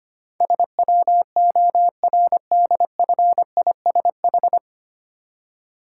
Morse code, heard as a keyed tone, about 25 wpm, 700 Hz. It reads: SWORDFISH
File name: practice / 25wpm / Difficult Words with Hints - 25wpm